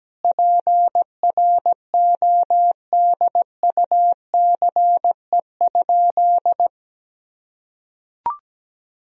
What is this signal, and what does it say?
Morse code, heard as a keyed tone, about 17 wpm, 700 Hz. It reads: PRODUCE?
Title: Sentences from Top 1000 Words - Review Part 3 - 17wpm